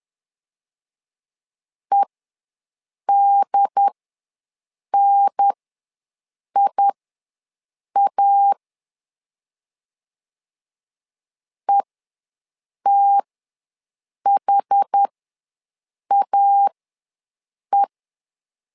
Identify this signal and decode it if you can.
Morse code, heard as a keyed tone, about 10 wpm, spaced as for 6 wpm, 800 Hz. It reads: EDNIA ETHAE
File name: LEZIONI / 2 UHDN / 2 UHDN ws 10wpm 2044